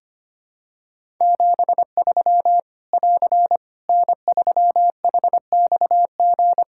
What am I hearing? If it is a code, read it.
Morse code, heard as a keyed tone, about 25 wpm, 700 Hz. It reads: 73 <AR> N3HXG
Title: Practice QSO - Exchange 4 - Ending - 25wpm